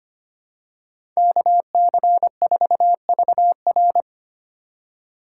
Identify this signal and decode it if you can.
Morse code, heard as a keyed tone, about 25 wpm, 700 Hz. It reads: KC4VR
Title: USA Call Signs - 5 Characters - 2x Character Spacing - 25wpm